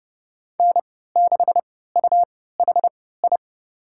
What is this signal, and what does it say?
Morse code, heard as a keyed tone, about 30 wpm, spaced as for 17 wpm, 700 Hz. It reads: N6UHI
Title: USA Call Signs - 5 Characters - 3x Character Spacing - 30wpm